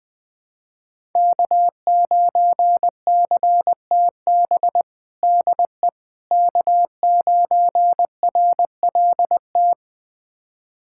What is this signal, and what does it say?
Morse code, heard as a keyed tone, about 20 wpm, 700 Hz. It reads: K9CTB DE K9RLT
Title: USA Callsign DE Callsign - 5 Characters - 20wpm